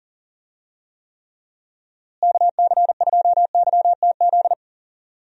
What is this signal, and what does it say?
Morse code, heard as a keyed tone, about 37 wpm, 700 Hz. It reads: KC2YTZ
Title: USA Call Signs - 6 Characters - Repeated 2x - 40wpm